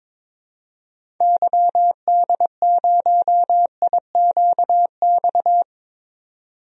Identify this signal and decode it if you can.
Morse code, heard as a keyed tone, about 22 wpm, 700 Hz. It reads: YD0IQX
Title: DXCC-100 Call Signs - 6 Characters - 22wpm